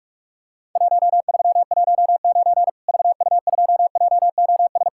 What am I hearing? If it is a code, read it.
Morse code, heard as a keyed tone, about 46 wpm, 700 Hz. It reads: 1319 VU2JOS
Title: Worked All Europe DX Contest - QTC Exchange - 45wpm